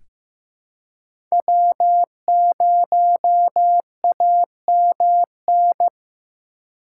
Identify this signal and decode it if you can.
Morse code, heard as a keyed tone, about 15 wpm, 700 Hz. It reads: W0AMN
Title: USA Call Signs - 5 Characters - 3x Character Spacing - 15wpm